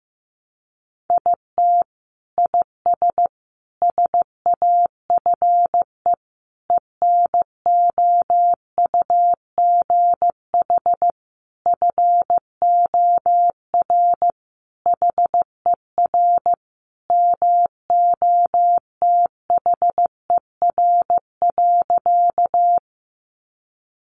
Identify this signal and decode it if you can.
Morse code, heard as a keyed tone, about 15 wpm, 700 Hz. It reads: IT IS SAFE ENOUGH FOR HER MOTHER.
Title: Sentences from Top 900 Words 15wpm